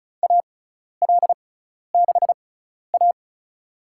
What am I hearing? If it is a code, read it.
Morse code, heard as a keyed tone, about 34 wpm, spaced as for 12 wpm, 700 Hz. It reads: AL6A
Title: USA Call Signs - 4 Characters - 6x Character Spacing - 35wpm